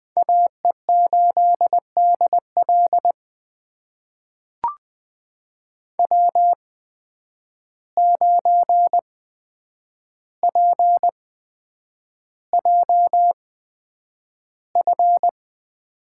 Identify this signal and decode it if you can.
Morse code, heard as a keyed tone, about 20 wpm, spaced as for 9 wpm, 700 Hz. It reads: AE8DL W9PJF
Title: USA Call Signs - 5 Characters - 8x Character Spacing - 20wpm